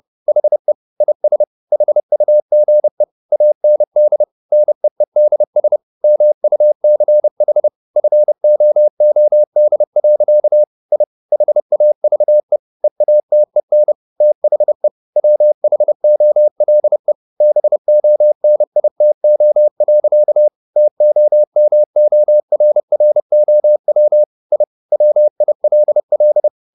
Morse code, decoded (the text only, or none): HE IS HUGE AND NEEDS MUCH FOOD. I HAVE EATEN THE WHOLE BONITO. TOMORROW I WILL